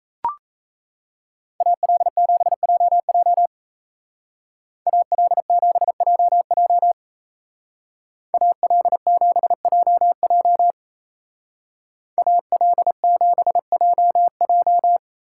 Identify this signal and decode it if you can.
Morse code, heard as a keyed tone, about 37 wpm, 700 Hz. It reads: AL7JJ AL7JJ AL7JJ AL7JJ